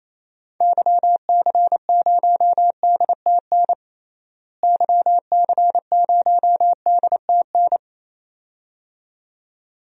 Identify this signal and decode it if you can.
Morse code, heard as a keyed tone, about 27 wpm, 700 Hz. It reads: YC0DTN YC0DTN
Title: DXCC-100 Call Signs - 6 Characters - Repeated 2x - 28wpm